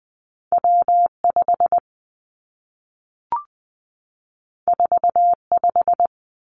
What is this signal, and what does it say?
Morse code, heard as a keyed tone, about 20 wpm, 700 Hz. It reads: W 5 4 5
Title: Two Challenging Letter-Number Combinations 20wpm